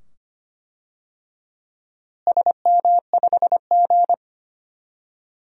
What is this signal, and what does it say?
Morse code, heard as a keyed tone, about 25 wpm, 700 Hz. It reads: SM5G
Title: DXCC-100 Call Signs - 4 Characters - Repeated 2x - 25wpm